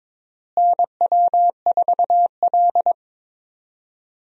Morse code, heard as a keyed tone, about 21 wpm, 700 Hz. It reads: NW4L
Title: USA Call Signs - 4 Characters - Repeated 3x - 22wpm